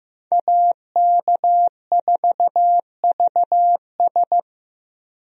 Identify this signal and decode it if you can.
Morse code, heard as a keyed tone, about 15 wpm, 700 Hz. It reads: AK4VS
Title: USA Call Signs - 5 Characters - Repeated 3x - 15wpm